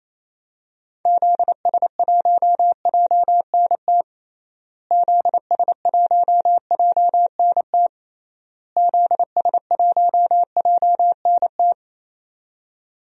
Morse code, heard as a keyed tone, about 28 wpm, 700 Hz. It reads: ZS1JNT ZS1JNT ZS1JNT
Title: DXCC-100 Call Signs - 6 Characters - Repeated 3x - 28wpm